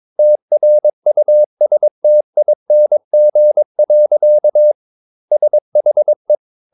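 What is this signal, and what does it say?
Morse code, heard as a keyed tone, about 22 wpm, 600 Hz. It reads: TRUSTING. SHE
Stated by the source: Morse code practice at 22 wpm, sending text from an English novel.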